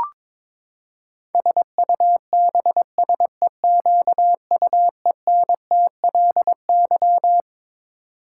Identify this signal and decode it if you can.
Morse code, heard as a keyed tone, about 22 wpm, 700 Hz. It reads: SUBSEQUENTLY
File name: practice / 22wpm / Twelve Letter Words - 22wpm